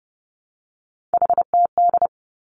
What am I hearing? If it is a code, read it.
Morse code, heard as a keyed tone, about 30 wpm, 700 Hz. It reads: HTD
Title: Three Challenging Letter-Number Combinations 30wpm